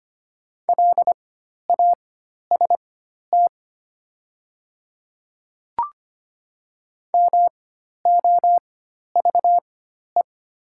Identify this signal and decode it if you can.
Morse code, heard as a keyed tone, about 25 wpm, spaced as for 12 wpm, 700 Hz. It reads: LAST MOVE
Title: 4x Character Spacing - Top 300 Words - 25wpm